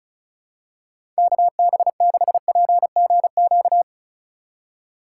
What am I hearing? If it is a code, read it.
Morse code, heard as a keyed tone, about 34 wpm, 700 Hz. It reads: KB6PGQ
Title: USA Call Signs - 6 Characters - 4x Character Spacing - 35wpm